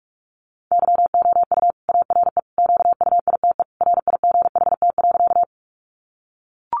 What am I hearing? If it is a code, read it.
Morse code, heard as a keyed tone, about 44 wpm, 700 Hz. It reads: YOU ARE QUITE RIGHT.
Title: Sentences from Top 700 Words 45wpm